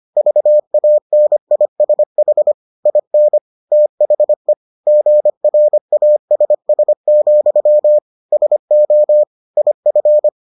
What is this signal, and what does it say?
Morse code, heard as a keyed tone, about 25 wpm, 600 Hz. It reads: VANISH IN THE GRASS, SO IF